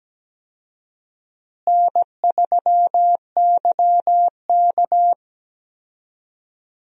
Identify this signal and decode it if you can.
Morse code, heard as a keyed tone, about 17 wpm, 700 Hz. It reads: N3YK
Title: USA Call Signs - 4 Characters - Repeated 2x - 17wpm